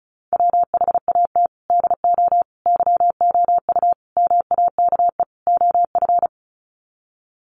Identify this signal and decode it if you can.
Morse code, heard as a keyed tone, about 34 wpm, 700 Hz. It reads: WHAT DO YOU MAKE OF
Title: Sets of 5 Words 35wpm